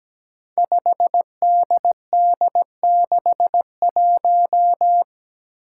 Morse code, heard as a keyed tone, about 17 wpm, 700 Hz. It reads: 5DD61
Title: Five Challenging Letter-Number Combinations 17wpm